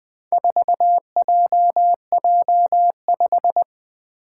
Morse code, heard as a keyed tone, about 20 wpm, 700 Hz. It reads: 4JJ5
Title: Four Challenging Letter-Number Combinations 20wpm